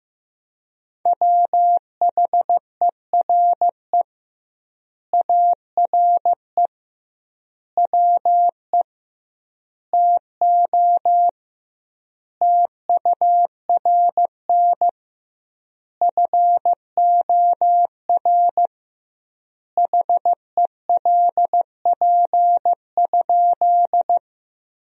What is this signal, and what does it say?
Morse code, heard as a keyed tone, about 15 wpm, 700 Hz. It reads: WHERE ARE WE TO TURN FOR HELP?
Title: 2x Word-Spacing - Sentences from Top 200 Words - 15wpm